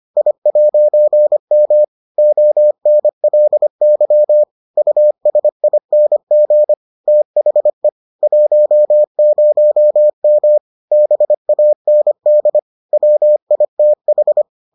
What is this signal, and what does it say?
Morse code, heard as a keyed tone, about 25 wpm, 600 Hz. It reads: I'M ONLY USING THE 10M BAND WITH